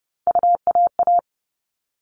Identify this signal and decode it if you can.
Morse code, heard as a keyed tone, about 30 wpm, 700 Hz. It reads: UAA
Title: All Cut Numbers - 3-Digit 30wpm